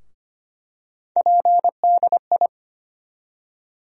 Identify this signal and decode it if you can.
Morse code, heard as a keyed tone, about 25 wpm, 700 Hz. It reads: PDI
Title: USA Call Sign Postfixes - 3 Letters 25wpm